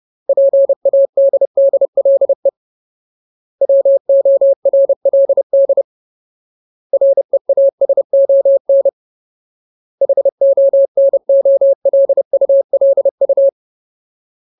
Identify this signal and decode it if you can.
Morse code, heard as a keyed tone, about 30 wpm, 550 Hz. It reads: PADDLE WORLD REASON HONOLULU
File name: Ultimate Word List / 30WPM Ultimate